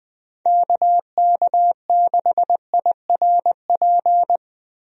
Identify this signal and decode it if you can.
Morse code, heard as a keyed tone, about 21 wpm, 700 Hz. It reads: KK6IRP
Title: USA Call Signs - 6 Characters - 20wpm